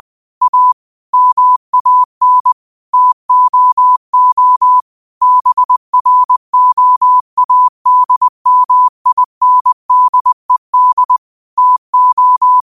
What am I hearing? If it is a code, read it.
Morse code, heard as a keyed tone, about 20 wpm, 1000 Hz. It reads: A MAN TOO BROADMINDED TO
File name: qotdfiles / 20wpm